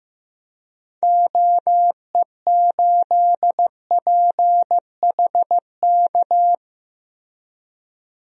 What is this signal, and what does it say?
Morse code, heard as a keyed tone, about 15 wpm, 700 Hz. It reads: OE8PHK
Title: DXCC-100 Call Signs - 6 Characters - Repeated 3x - 15wpm